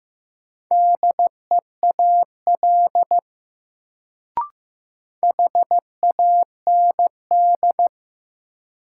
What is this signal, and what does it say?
Morse code, heard as a keyed tone, about 15 wpm, 700 Hz. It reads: DEAL HAND
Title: Sets of 3 Words - New Word Drill - 15wpm